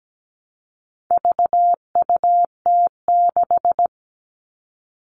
Morse code, heard as a keyed tone, about 17 wpm, 700 Hz. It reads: VUT6
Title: All Cut Numbers - 4-Digit 17wpm